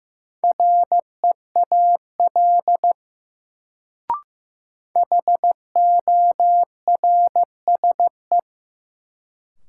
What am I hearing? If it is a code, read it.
Morse code, heard as a keyed tone, about 15 wpm, 700 Hz. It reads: REAL HORSE
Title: Sentences from Top 300 Words - New Word Drill - 15wpm